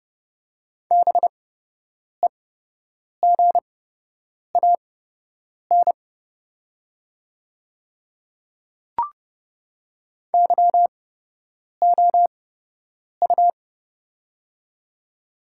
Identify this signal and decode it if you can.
Morse code, heard as a keyed tone, about 30 wpm, spaced as for 8 wpm, 690 Hz. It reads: BEGAN YOU
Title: 8x Character Spacing - Top 500 Words - 30wpm